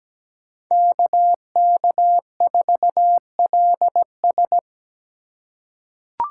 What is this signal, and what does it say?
Morse code, heard as a keyed tone, about 17 wpm, 700 Hz. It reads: KK4LS
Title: USA Call Signs - 5 Characters - 2x Character Spacing - 17wpm